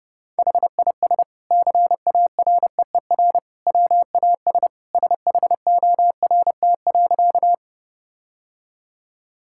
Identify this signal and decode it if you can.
Morse code, heard as a keyed tone, about 30 wpm, 700 Hz. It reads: HIS CAREER WAS SHORT.
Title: Sentences from Top 1000 Words - Review Part 6 - 30wpm